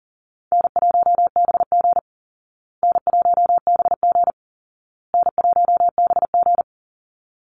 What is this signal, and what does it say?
Morse code, heard as a keyed tone, about 40 wpm, 700 Hz. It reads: N1BG N1BG N1BG
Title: Call Signs Repeated 2x - 40wpm